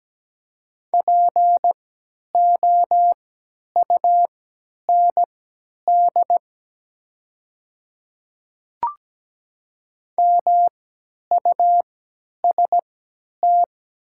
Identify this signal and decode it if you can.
Morse code, heard as a keyed tone, about 17 wpm, spaced as for 10 wpm, 700 Hz. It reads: POUND MUST
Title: 3x Character Spacing - Top 500 Words - 17wpm